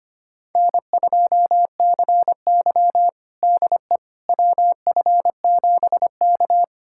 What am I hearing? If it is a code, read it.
Morse code, heard as a keyed tone, about 25 wpm, 700 Hz. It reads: N2CY DE WF7K
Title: USA Callsign DE Callsign - 4 Characters - 25wpm